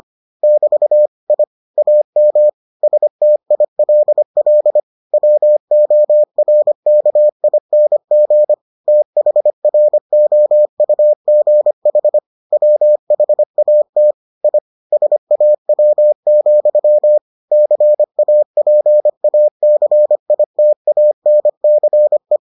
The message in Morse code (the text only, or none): <BT> I AM STILL WORKING THROUGH WHAT I SAW, CAPACITANCE